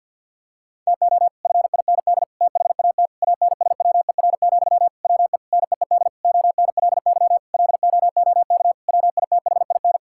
Morse code, heard as a keyed tone, about 50 wpm, 700 Hz. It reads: TO FIND THAT ANSWER, WE NEED ONLY LOOK WITHIN
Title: Sentences from Top 1600 Words 50wpm